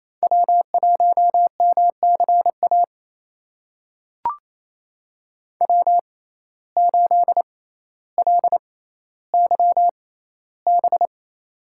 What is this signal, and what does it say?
Morse code, heard as a keyed tone, about 28 wpm, spaced as for 14 wpm, 700 Hz. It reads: W1MCA W8LYB
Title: USA Call Signs - 6x Character Spacing - 28wpm